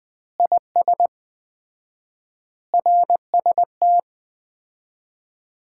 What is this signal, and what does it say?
Morse code, heard as a keyed tone, about 20 wpm, 700 Hz. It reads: IS RST
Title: Common QSO Phrases - 4x Word-Spacing - 20wpm